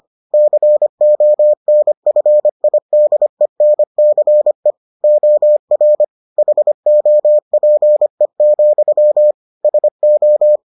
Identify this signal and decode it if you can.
Morse code, heard as a keyed tone, about 25 wpm, 600 Hz. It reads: CONFIDENCE OR HOPE, SO